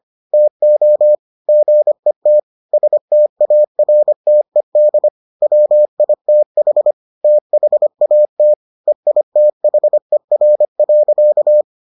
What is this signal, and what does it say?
Morse code, heard as a keyed tone, about 25 wpm, 600 Hz. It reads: TO GET STARTED WITH THAT EITHER.